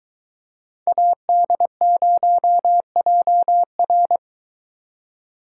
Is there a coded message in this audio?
Yes: Morse code from 0.9 to 4.2 s, heard as a keyed tone, about 22 wpm, 700 Hz. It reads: AD0JR